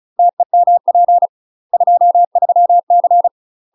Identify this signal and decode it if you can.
Morse code, heard as a keyed tone, about 35 wpm, 700 Hz. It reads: TEMP 23C
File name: UK MASTER QSOs TRAINING / 35WPM / UK MASTER QSOs 0032